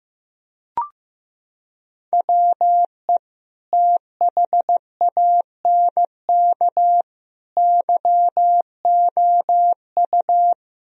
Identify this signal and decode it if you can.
Morse code, heard as a keyed tone, about 15 wpm, 700 Hz. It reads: WE THANK YOU
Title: Sentences from Top 1000 Words - Review Part 1 - 15wpm